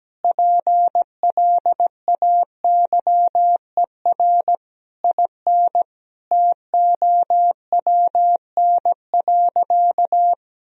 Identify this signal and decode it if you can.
Morse code, heard as a keyed tone, about 17 wpm, 700 Hz. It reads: PLAYER IN TOWN.
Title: Sentences from Top 1000 Words - Review Part 4 - 17wpm